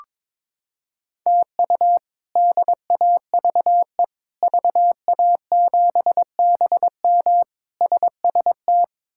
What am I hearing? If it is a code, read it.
Morse code, heard as a keyed tone, about 22 wpm, 700 Hz. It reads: TU DAVE VA7BM SST
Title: K1USN SST QSO Practice - 22wpm